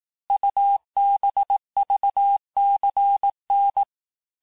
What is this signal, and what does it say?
Morse code, heard as a keyed tone, about 18 wpm, 800 Hz. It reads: UBVCN
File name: cours-du-lundi files / cours-1852-texte-chiffre-vitesse-17MPM